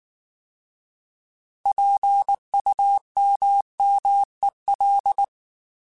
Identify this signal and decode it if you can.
Morse code, heard as a keyed tone, about 19 wpm, 800 Hz. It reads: PUMMEL
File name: Alphabetical / Alphabetical mp3s / Ps @18